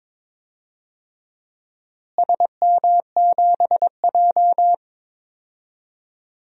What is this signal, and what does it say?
Morse code, heard as a keyed tone, about 22 wpm, 700 Hz. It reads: SM7J
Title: DXCC-100 Call Signs - 4 Characters - Repeated 2x - 22wpm